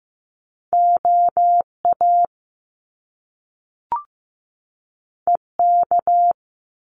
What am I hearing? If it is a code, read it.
Morse code, heard as a keyed tone, about 15 wpm, 700 Hz. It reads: OA EK
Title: Top Two Letter Combinations in Words 15wpm